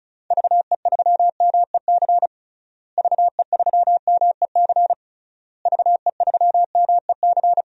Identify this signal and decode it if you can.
Morse code, heard as a keyed tone, about 34 wpm, 700 Hz. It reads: VE3MEC VE3MEC VE3MEC
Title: DXCC-100 Call Signs - 6 Characters - Repeated 3x - 35wpm